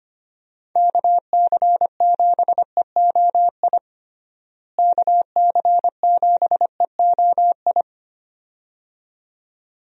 Morse code, heard as a keyed tone, about 25 wpm, 700 Hz. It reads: KC7EOI KC7EOI
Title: USA Call Signs - 6 Characters - Repeated 2x - 25wpm